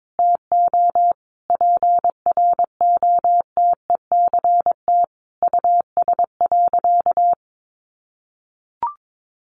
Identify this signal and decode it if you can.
Morse code, heard as a keyed tone, about 22 wpm, 700 Hz. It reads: TO PROTECT US.
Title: Sentences from Top 900 Words 22wpm